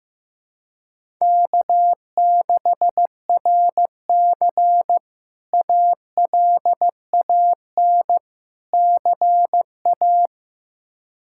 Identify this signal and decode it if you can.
Morse code, heard as a keyed tone, about 15 wpm, 700 Hz. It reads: K6RC ALAN CA
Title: K1USN SST QSO Practice - 15wpm